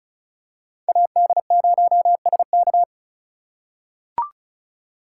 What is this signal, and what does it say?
Morse code, heard as a keyed tone, about 36 wpm, 690 Hz. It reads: AD0SK
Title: USA Call Signs - 5 Characters - 3x Character Spacing - 35wpm